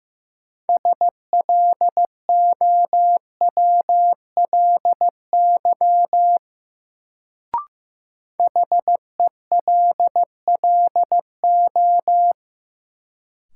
Morse code, heard as a keyed tone, about 16 wpm, 700 Hz. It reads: SLOWLY HELLO
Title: Sentences from Top 1100 Words - New Word Drill - 15wpm